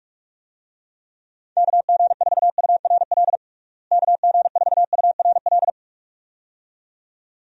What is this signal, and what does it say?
Morse code, heard as a keyed tone, about 46 wpm, 700 Hz. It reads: KG4URL KG4URL
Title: USA Call Signs - 6 Characters - Repeated 2x - 45wpm